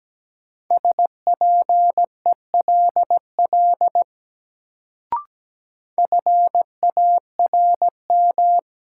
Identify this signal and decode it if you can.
Morse code, heard as a keyed tone, about 17 wpm, 700 Hz. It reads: SPELL FARM
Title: Sentences from Top 1200 Words - New Word Drill Alt - 17wpm